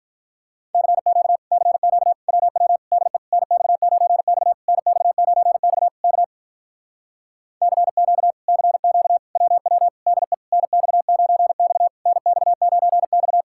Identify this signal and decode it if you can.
Morse code, heard as a keyed tone, about 46 wpm, 700 Hz. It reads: CQ CQ WW DE NX9X NX9X K CQ CQ WW DE NX9X NX9X